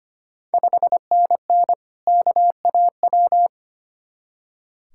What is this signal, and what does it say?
Morse code, heard as a keyed tone, about 25 wpm, 700 Hz. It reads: 5NN KAW
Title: Ontario QSO Party Exchange 25wpm